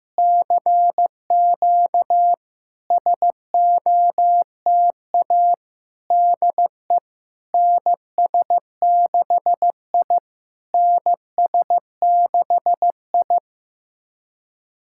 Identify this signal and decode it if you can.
Morse code, heard as a keyed tone, about 16 wpm, 700 Hz. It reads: CQ SOTA DE NS6I NS6I
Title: SOTA QSO Practice 15wpm